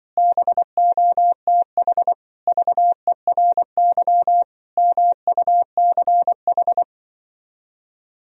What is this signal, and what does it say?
Morse code, heard as a keyed tone, about 25 wpm, 700 Hz. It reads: BOTH VERY MUCH
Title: Speed-Racing - Sets of 5 Words 20wpm